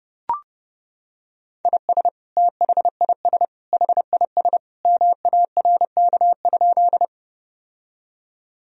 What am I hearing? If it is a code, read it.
Morse code, heard as a keyed tone, about 30 wpm, 700 Hz. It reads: IS THIS HIS MARK?
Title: Sentences from Top 1000 Words - Review Part 1 - 30wpm